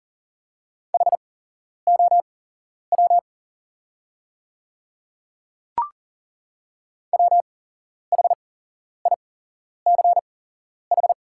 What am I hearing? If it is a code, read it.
Morse code, heard as a keyed tone, about 43 wpm, spaced as for 11 wpm, 690 Hz. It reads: HOW WHICH
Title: 8x Character Spacing - Top 200 Words - 40wpm